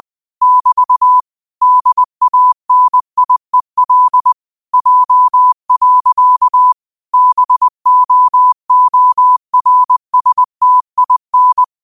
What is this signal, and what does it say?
Morse code, heard as a keyed tone, about 20 wpm, 1000 Hz. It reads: <BT> DANIEL J. BOORSTIN
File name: qotdfiles / 20wpm